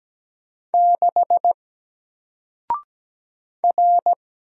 Morse code, heard as a keyed tone, about 17 wpm, 700 Hz. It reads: 6 R
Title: Single Letter-Number Alt 17wpm